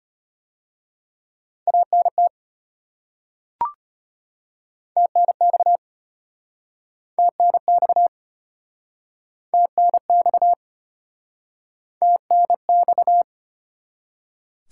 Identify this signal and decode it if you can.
Morse code, heard as a keyed tone, about 34 wpm, 700 Hz. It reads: ANT TNX TNX TNX TNX